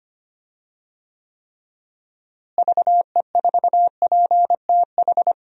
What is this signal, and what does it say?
Morse code, heard as a keyed tone, about 25 wpm, 700 Hz. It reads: VE4PTH
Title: DXCC-100 Call Signs - 6 Characters - Repeated 2x - 25wpm